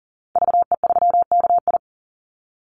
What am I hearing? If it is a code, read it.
Morse code, heard as a keyed tone, about 40 wpm, 700 Hz. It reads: VE3KI
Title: Call Signs - 40wpm